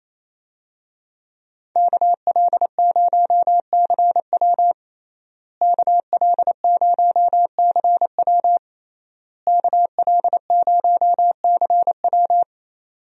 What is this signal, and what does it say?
Morse code, heard as a keyed tone, about 28 wpm, 700 Hz. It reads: KL0CW KL0CW KL0CW
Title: USA Call Signs - 5 Characters - Repeated 3x - 28wpm